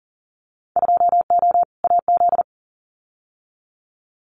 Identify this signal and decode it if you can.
Morse code, heard as a keyed tone, about 40 wpm, 700 Hz. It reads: 2O AZ
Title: Winter Field Day Exchanges 40wpm